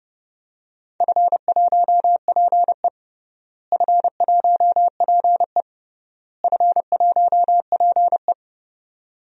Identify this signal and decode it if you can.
Morse code, heard as a keyed tone, about 30 wpm, 700 Hz. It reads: F1PE F1PE F1PE
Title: DXCC-100 Call Signs - 4 Characters - Repeated 3x - 30wpm